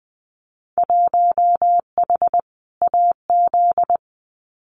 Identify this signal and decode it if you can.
Morse code, heard as a keyed tone, about 20 wpm, 700 Hz. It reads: 1H AZ
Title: Winter Field Day Exchanges 20wpm